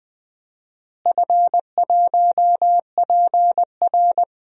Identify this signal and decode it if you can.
Morse code, heard as a keyed tone, about 21 wpm, 700 Hz. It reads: F1PR
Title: DXCC-100 Call Signs - 4 Characters - Repeated 3x - 20wpm